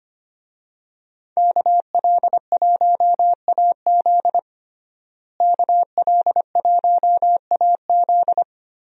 Morse code, heard as a keyed tone, about 25 wpm, 700 Hz. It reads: KL1AZ KL1AZ
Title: USA Call Signs - 5 Characters - Repeated 3x - 25wpm